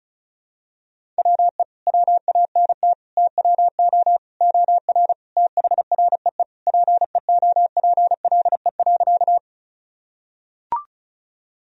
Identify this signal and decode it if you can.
Morse code, heard as a keyed tone, about 36 wpm, 700 Hz. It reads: WE WANT TWO OR THREE PEOPLE.
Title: Sentences from Top 1000 Words - Review Part 1 - 35wpm